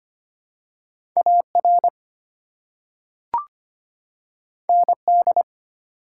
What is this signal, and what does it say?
Morse code, heard as a keyed tone, about 25 wpm, 700 Hz. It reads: AR ND
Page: morse-code-ninja-practice-25wpm